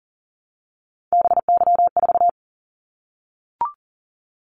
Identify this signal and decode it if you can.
Morse code, heard as a keyed tone, about 40 wpm, 700 Hz. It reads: BY4
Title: Callsign Prefixes for Top 100 DXCC - 40wpm